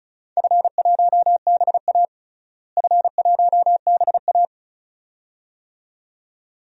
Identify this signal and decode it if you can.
Morse code, heard as a keyed tone, about 34 wpm, 700 Hz. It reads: F1BA F1BA
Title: DXCC-100 Call Signs - 4 Characters - Repeated 2x - 35wpm